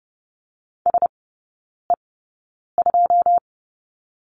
Morse code, heard as a keyed tone, about 30 wpm, spaced as for 9 wpm, 700 Hz. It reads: SE2
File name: 081 Lesson 14e - Number 2 - Three Characters 30wpm